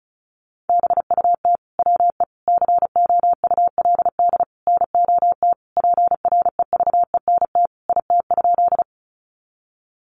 Morse code, heard as a keyed tone, about 34 wpm, 700 Hz. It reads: BUT WE COULD NOT PREVENT IT?
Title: Sentences from Top 1100 Words 35wpm